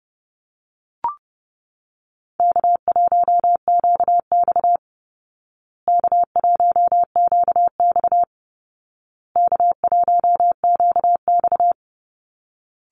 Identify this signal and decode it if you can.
Morse code, heard as a keyed tone, about 30 wpm, 700 Hz. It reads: K1QX K1QX K1QX
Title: Call Signs Repeated 2x - 30wpm